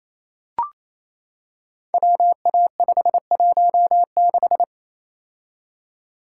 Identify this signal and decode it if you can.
Morse code, heard as a keyed tone, about 27 wpm, 700 Hz. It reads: WA516
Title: Code Groups - 5 Letter:Numbers - 28wpm